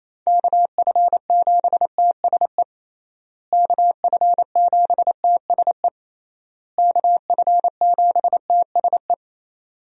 Morse code, heard as a keyed tone, about 28 wpm, 700 Hz. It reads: KF7TSE KF7TSE KF7TSE
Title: USA Call Signs - 6 Characters - Repeated 3x - 28wpm